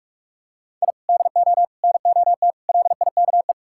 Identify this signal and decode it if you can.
Morse code, heard as a keyed tone, about 46 wpm, 700 Hz. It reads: I DO NOT LIKE
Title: Sentences from Top 1000 Words - Review Part 6 - 45wpm